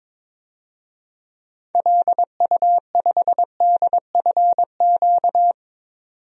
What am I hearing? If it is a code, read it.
Morse code, heard as a keyed tone, about 22 wpm, 700 Hz. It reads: LU5DFQ
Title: DXCC-100 Call Signs - 6 Characters - 22wpm